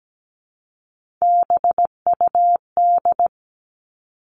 Morse code, heard as a keyed tone, about 17 wpm, 700 Hz. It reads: BUD
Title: All Cut Numbers - 3-Digit 17wpm